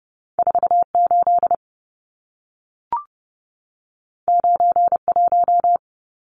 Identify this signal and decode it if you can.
Morse code, heard as a keyed tone, about 30 wpm, 700 Hz. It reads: 48 91
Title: Numbers - 2-Digit 30wpm